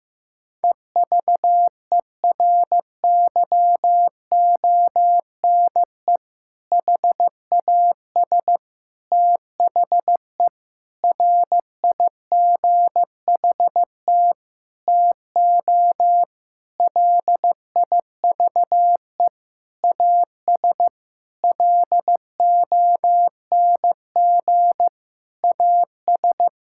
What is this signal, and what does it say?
Morse code, heard as a keyed tone, about 16 wpm, 700 Hz. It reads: EVERYONE HAS THE RIGHT TO LIVE AS LONG AS
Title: Sentences from Top 1000 Words - Review Part 5 - 15wpm